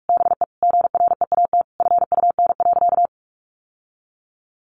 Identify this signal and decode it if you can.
Morse code, heard as a keyed tone, about 44 wpm, 700 Hz. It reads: BE GREAT FUN.
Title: Sentences from Top 700 Words 45wpm